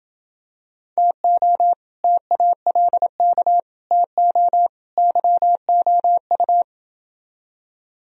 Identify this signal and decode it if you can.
Morse code, heard as a keyed tone, about 28 wpm, 700 Hz. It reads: TO TALK TO YOU